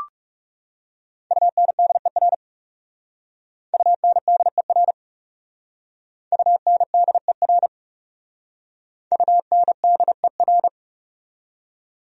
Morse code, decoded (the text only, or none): UNDER UNDER UNDER UNDER